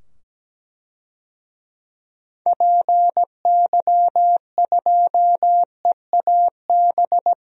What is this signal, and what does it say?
Morse code, heard as a keyed tone, about 17 wpm, 700 Hz. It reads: PY2EAB
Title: DXCC-100 Call Signs - 6 Characters - 17wpm